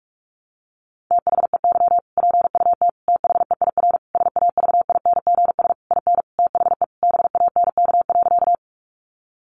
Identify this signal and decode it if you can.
Morse code, heard as a keyed tone, about 44 wpm, 700 Hz. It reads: THEY PUT THEIR SAVINGS IN THE BANK.
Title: Sentences from Top 1200 Words 45wpm